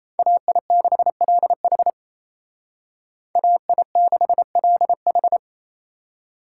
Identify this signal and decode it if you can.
Morse code, heard as a keyed tone, about 30 wpm, 700 Hz. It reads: AI6LH AI6LH
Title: Speed Racing - USA 2x2 Call Signs 28wpm